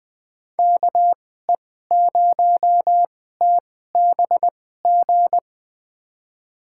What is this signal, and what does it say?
Morse code, heard as a keyed tone, about 20 wpm, spaced as for 14 wpm, 700 Hz. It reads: KE0TBG
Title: USA Call Signs - 6 Characters - 2x Character Spacing - 20wpm